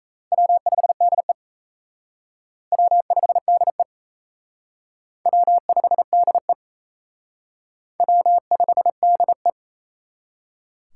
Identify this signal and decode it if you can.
Morse code, heard as a keyed tone, about 37 wpm, 700 Hz. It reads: W5DE W5DE W5DE W5DE